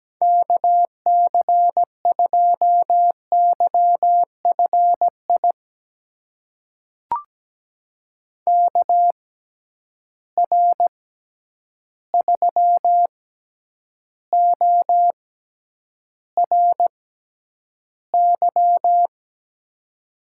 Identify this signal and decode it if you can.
Morse code, heard as a keyed tone, about 17 wpm, spaced as for 9 wpm, 700 Hz. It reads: KC2YFI KR3ORY
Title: USA Call Signs - 6 Characters - 6x Character Spacing - 17wpm